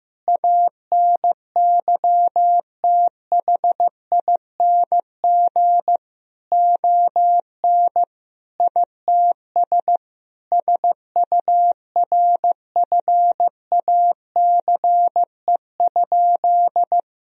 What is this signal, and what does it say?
Morse code, heard as a keyed tone, about 16 wpm, 700 Hz. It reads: ANYTHING ON ITS SURFACE?
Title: Sentences from Top 1000 Words - Review Part 3 - 15wpm